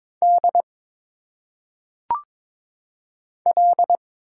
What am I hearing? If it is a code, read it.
Morse code, heard as a keyed tone, about 22 wpm, 700 Hz. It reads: D L
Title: Single Letters - Phonetic Pronunciation 22wpm